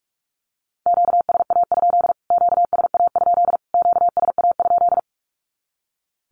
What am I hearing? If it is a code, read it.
Morse code, heard as a keyed tone, about 44 wpm, 700 Hz. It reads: QSA? QSA? QSA?
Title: Q-Codes Repeated 3x 45wpm